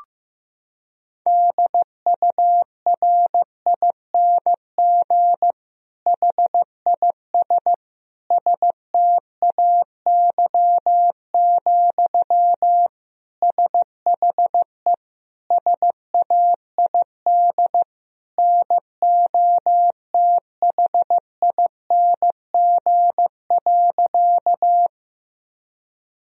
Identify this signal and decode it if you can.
Morse code, heard as a keyed tone, about 15 wpm, 700 Hz. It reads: DURING HIS STAY, SHE SAID NOTHING.
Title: Sentences from Top 1000 Words - Review Part 2 - 15wpm